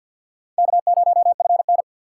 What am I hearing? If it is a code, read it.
Morse code, heard as a keyed tone, about 51 wpm, 690 Hz. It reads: K0FN
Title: USA Call Signs - 4 Characters - 6x Character Spacing - 50wpm